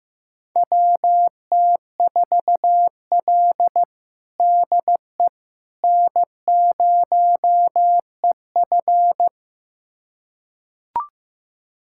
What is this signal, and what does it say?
Morse code, heard as a keyed tone, about 15 wpm, 700 Hz. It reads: WT4L DE N0EF
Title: USA Callsign DE Callsign - 4 Characters - 15wpm